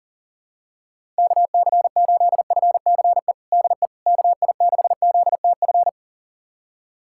Morse code, heard as a keyed tone, about 40 wpm, 700 Hz. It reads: KC8FCE DE KI6ZTF
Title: USA Callsign DE Callsign - 6 Characters - 40wpm